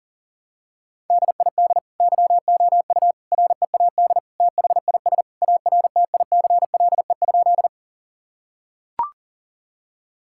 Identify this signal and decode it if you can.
Morse code, heard as a keyed tone, about 37 wpm, 700 Hz. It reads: DID YOU READ THIS ARTICLE?
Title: Sentences from Top 1000 Words - Review Part 5 - 40wpm